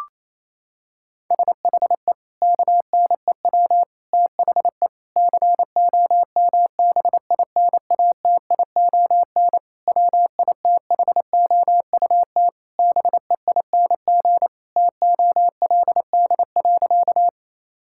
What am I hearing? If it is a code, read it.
Morse code, heard as a keyed tone, about 28 wpm, 700 Hz. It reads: SHE KNEW THE COMBINATION WITHOUT BEING TOLD.
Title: Sentences from Top 1000 Words - Review Part 6 - 28wpm